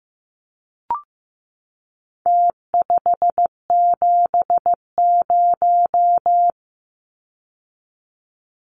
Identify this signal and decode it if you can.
Morse code, heard as a keyed tone, about 15 wpm, 700 Hz. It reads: T570
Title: Numbers - Contest Style - 4-Digit 15wpm